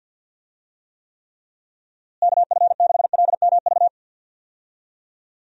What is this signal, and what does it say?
Morse code, heard as a keyed tone, about 52 wpm, 700 Hz. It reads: KF6LMV
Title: USA Call Signs - 6 Characters - Repeated 3x - 50wpm